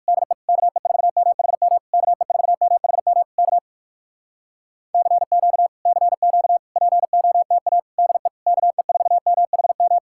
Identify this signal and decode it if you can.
Morse code, heard as a keyed tone, about 58 wpm, 700 Hz. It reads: DE KE4MHM KE4MHM K CQ CQ POTA DE KE4MHM